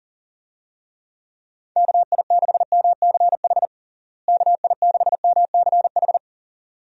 Morse code, heard as a keyed tone, about 40 wpm, 700 Hz. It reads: KI6MCH KI6MCH
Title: USA Call Signs - 6 Characters - Repeated 2x - 40wpm